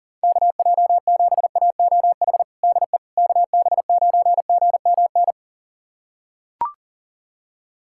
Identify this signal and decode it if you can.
Morse code, heard as a keyed tone, about 40 wpm, 700 Hz. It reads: KJ7AOH DE KB9GMN
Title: USA Callsign DE Callsign - 6 Characters - 40wpm